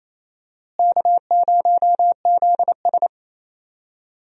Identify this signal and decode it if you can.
Morse code, heard as a keyed tone, about 28 wpm, 700 Hz. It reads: K0ZS
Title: USA Call Signs - 4 Characters - Repeated 2x - 28wpm